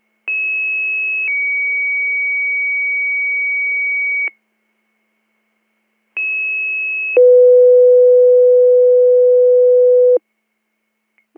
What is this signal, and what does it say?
Two-tone page, heard as a one-second tone, then a three-second tone, about 2600 then 2250 Hz, about 2600 then 500 Hz. It is heard from 0.3 to 10.2 s.